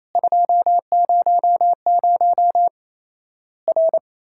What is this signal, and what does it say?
Morse code, heard as a keyed tone, about 27 wpm, 700 Hz. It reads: 200 R
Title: Worked All Europe DX Contest - QTC Exchange - 28wpm